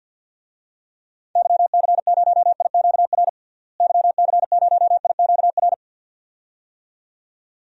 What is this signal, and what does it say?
Morse code, heard as a keyed tone, about 47 wpm, 700 Hz. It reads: YC0IQR YC0IQR
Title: DXCC-100 Call Signs - 6 Characters - Repeated 2x - 50wpm